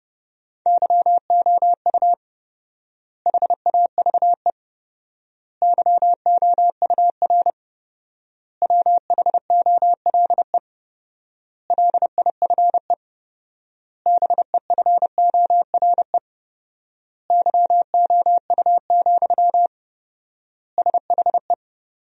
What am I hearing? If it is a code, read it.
Morse code, heard as a keyed tone, about 32 wpm, 700 Hz. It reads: YOU HAVE YOUR WHOLE LIFE BEFORE YOU, SHE
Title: 4x Word-Spacing - Sentences from Top 400 Words - 30wpm